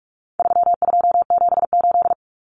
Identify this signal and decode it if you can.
Morse code, heard as a keyed tone, about 44 wpm, 700 Hz. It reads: 3278
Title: Common Cut Numbers - 4-Digit 45wpm